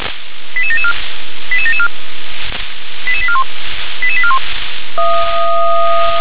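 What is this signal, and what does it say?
Five-tone selective call (ZVEI), heard as quick runs of five tones, tones of about 70 ms. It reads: 88774 88774 88841 88841
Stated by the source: ZVEI five-tone call received by SDR.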